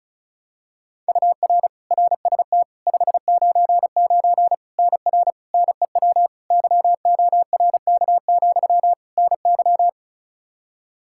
Morse code, heard as a keyed tone, about 36 wpm, 700 Hz. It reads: UR RST 599 NR NEW YORK, NY
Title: K1USN SST QSO Practice 35wpm